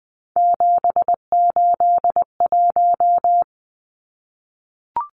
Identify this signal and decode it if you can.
Morse code, heard as a keyed tone, about 20 wpm, 700 Hz. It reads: 781
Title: Numbers - 3-Digit 20wpm